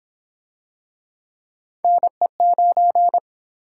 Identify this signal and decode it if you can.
Morse code, heard as a keyed tone, about 30 wpm, 700 Hz. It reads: NE9